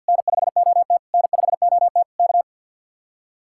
Morse code, heard as a keyed tone, about 61 wpm, 700 Hz. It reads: N5OT N5OT K
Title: Speed-Racing Common QSO Phrases - 50wpm